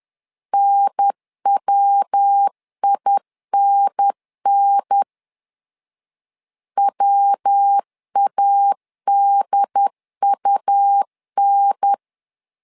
Morse code, heard as a keyed tone, about 10 wpm, 800 Hz. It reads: NWINN WADUN
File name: LEZIONI / 3 MWRO / 3 MW 10wpm 1959